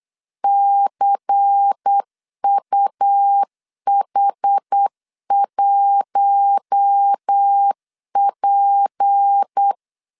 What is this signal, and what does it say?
Morse code, heard as a keyed tone, about 8 wpm, 800 Hz. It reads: CUH1P
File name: LEZIONI / 6 PKX0 / 6 P 8wpm 3025